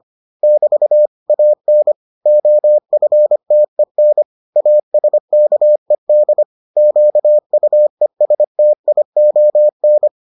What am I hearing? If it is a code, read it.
Morse code, heard as a keyed tone, about 25 wpm, 600 Hz. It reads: <BT> AN OFTEN ASKED QUESTION